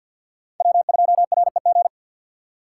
Morse code, heard as a keyed tone, about 47 wpm, 700 Hz. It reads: W2REG
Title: USA Call Signs - 5 Characters - 6x Character Spacing - 50wpm